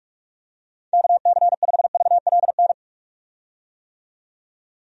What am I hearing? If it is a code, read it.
Morse code, heard as a keyed tone, about 47 wpm, 700 Hz. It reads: KC5VLN
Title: USA Call Signs - 6 Characters - 45wpm